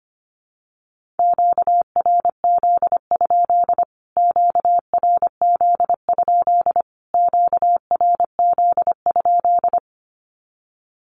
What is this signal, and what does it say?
Morse code, heard as a keyed tone, about 25 wpm, 700 Hz. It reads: QRZ? QRZ? QRZ?
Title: Q-Codes Repeated 3x 25wpm